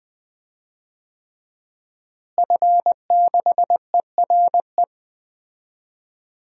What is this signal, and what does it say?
Morse code, heard as a keyed tone, about 20 wpm, 700 Hz. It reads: F6ERE
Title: DXCC-100 Call Signs - 5 Characters - Repeated 3x - 20wpm